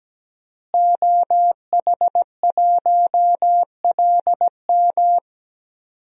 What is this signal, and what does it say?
Morse code, heard as a keyed tone, about 17 wpm, 700 Hz. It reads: OH1LM
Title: DXCC-100 Call Signs - 5 Characters - Repeated 2x - 17wpm